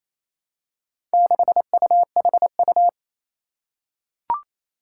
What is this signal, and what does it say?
Morse code, heard as a keyed tone, about 28 wpm, 700 Hz. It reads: 6UHU
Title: Four Challenging Letter-Number Combinations 28wpm